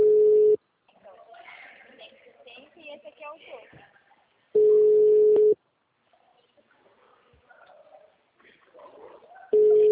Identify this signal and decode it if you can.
Ringback tone (Europe, 425 Hz), with speech.